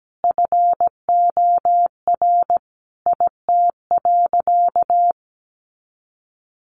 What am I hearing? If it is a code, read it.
Morse code, heard as a keyed tone, about 17 wpm, 700 Hz. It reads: FOR IT.
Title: Sentences from Top 300 Words 17wpm